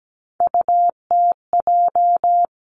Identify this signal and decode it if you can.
Morse code, heard as a keyed tone, about 17 wpm, 700 Hz. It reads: UTJ
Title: Three Challenging Letter-Number Combinations 17wpm